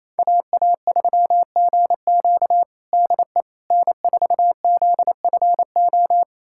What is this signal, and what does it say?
Morse code, heard as a keyed tone, about 28 wpm, 700 Hz. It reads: AA3GQ DE N4ZFO
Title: USA Callsign DE Callsign - 5 Characters - 28wpm